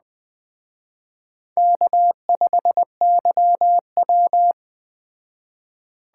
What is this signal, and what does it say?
Morse code, heard as a keyed tone, about 20 wpm, 700 Hz. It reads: K5YW
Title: Speed Racing - USA 1x2 Call Signs 17wpm